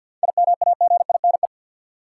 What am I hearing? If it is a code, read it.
Morse code, heard as a keyed tone, about 53 wpm, 700 Hz. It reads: IMAGINE
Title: Seven Letter Words - 50wpm